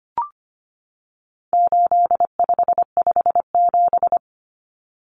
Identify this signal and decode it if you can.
Morse code, heard as a keyed tone, about 25 wpm, 700 Hz. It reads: 8557
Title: Numbers - 4-Digit 25wpm